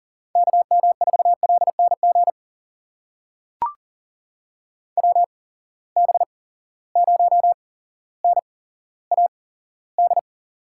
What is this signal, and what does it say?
Morse code, heard as a keyed tone, about 40 wpm, spaced as for 17 wpm, 700 Hz. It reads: KM4LNG WB0NAD
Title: USA Call Signs - 6 Characters - 8x Character Spacing - 40wpm